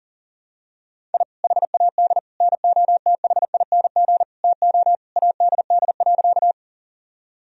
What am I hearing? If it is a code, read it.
Morse code, heard as a keyed tone, about 40 wpm, 700 Hz. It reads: I HAD NOTHING TO ADD.
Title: Sentences from Top 1000 Words - Review Part 1 - 40wpm